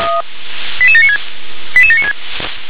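Five-tone selective call (ZVEI), heard as quick runs of five tones, tones of about 70 ms. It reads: 88796 88796
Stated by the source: Received on an SDR, ZVEI five-tone selective call, code 88792.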